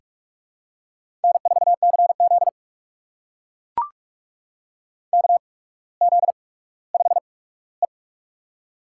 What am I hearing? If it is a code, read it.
Morse code, heard as a keyed tone, about 44 wpm, spaced as for 19 wpm, 700 Hz. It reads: N4CZ KZ5E
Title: USA Call Signs - 8x Character Spacing - 45wpm